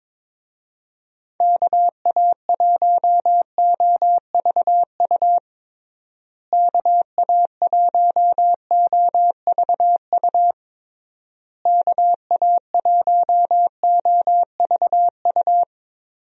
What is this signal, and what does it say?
Morse code, heard as a keyed tone, about 22 wpm, 700 Hz. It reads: KA1OVU KA1OVU KA1OVU
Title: USA Call Signs - 6 Characters - Repeated 3x - 22wpm